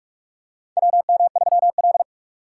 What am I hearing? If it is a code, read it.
Morse code, heard as a keyed tone, about 42 wpm, 700 Hz. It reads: WM3L
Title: USA Call Signs - 4 Characters - 4x Character Spacing - 45wpm